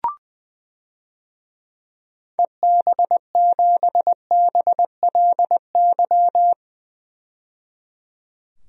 Morse code, heard as a keyed tone, about 21 wpm, 700 Hz. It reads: EB7BLY
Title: DXCC-100 Call Signs - 6 Characters - 20wpm